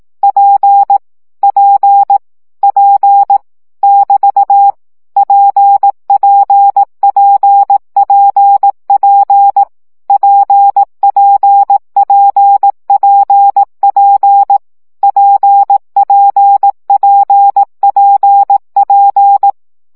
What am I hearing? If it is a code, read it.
Morse code, heard as a keyed tone, about 18 wpm, 800 Hz. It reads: P P P <BT> PPPPP PPPPP PPPPP
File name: CW gyakorlas p 15 baud